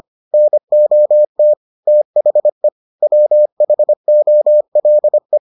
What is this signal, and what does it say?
Morse code, heard as a keyed tone, about 25 wpm, 600 Hz. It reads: NOT THE WHOLE